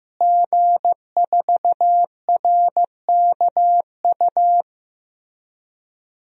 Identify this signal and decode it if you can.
Morse code, heard as a keyed tone, about 15 wpm, 700 Hz. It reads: G4RKU
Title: DXCC-100 Call Signs - 5 Characters - Repeated 2x - 15wpm